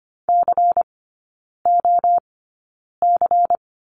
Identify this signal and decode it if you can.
Morse code, heard as a keyed tone, about 25 wpm, spaced as for 9 wpm, 700 Hz. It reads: COC
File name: course / 25:10wpm / 093 Lesson 16e - Letter c - Three Characters 25wpm